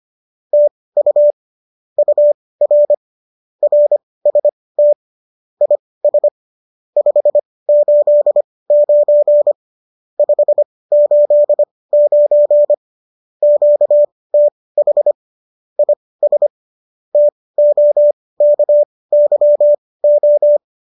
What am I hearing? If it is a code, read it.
Morse code, heard as a keyed tone, about 25 wpm, spaced as for 18 wpm, 590 Hz. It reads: TU UR RST IS 589 589 QTH IS TOKYO